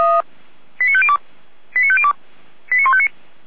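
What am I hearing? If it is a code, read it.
Five-tone selective call (ZVEI), heard as quick runs of five tones, tones of about 70 ms. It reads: 89502 89502 89159
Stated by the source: ZVEI five-tone call received by SDR.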